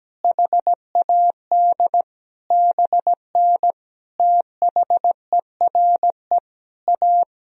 Morse code, heard as a keyed tone, about 17 wpm, 700 Hz. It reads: HAD BN THERE A
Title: Sentences from Top 400 Words and Abbreviations 17wpm